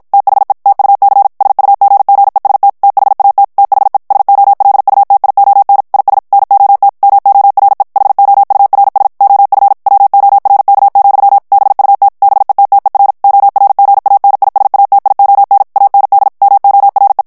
Morse code, heard as a keyed tone, about 54 wpm, 800 Hz. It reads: THE TUC SUGGEST THAT THE SOLUTION IS NOT MORE HOURS OF WORK, BUT BETTER ORGANISATION AND MORE
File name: morse mp3 / ANGLAIS texte clair 1 vitesse 50 mots minute